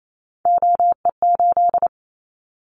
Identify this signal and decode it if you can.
Morse code, heard as a keyed tone, about 28 wpm, 700 Hz. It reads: OE8
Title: Callsign Prefixes for Top 100 DXCC - 28wpm